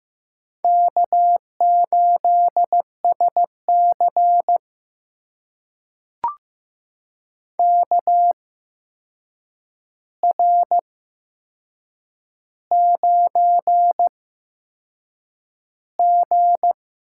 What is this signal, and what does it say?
Morse code, heard as a keyed tone, about 15 wpm, spaced as for 6 wpm, 700 Hz. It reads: K8SC KR9G
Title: USA Call Signs - 4 Characters - 8x Character Spacing - 15wpm